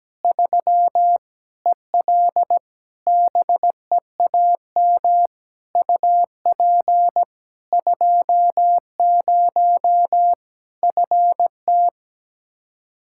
Morse code, heard as a keyed tone, about 17 wpm, 700 Hz. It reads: 3 EL BEAM UP 20 FT
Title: Practice QSO - Exchange 2 - Info - 17wpm